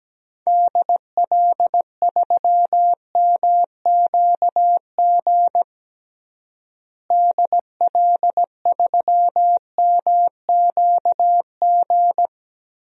Morse code, heard as a keyed tone, about 17 wpm, 700 Hz. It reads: DL3MQG DL3MQG
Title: DXCC-100 Call Signs - 6 Characters - Repeated 3x - 17wpm